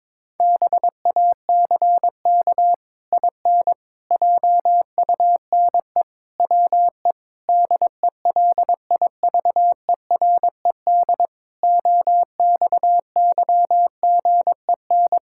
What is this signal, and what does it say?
Morse code, heard as a keyed tone, about 22 wpm, 700 Hz. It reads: BACK IN JUNE WE DELIVERED OXYGEN
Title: Pangram Challenge v2 - 22wpm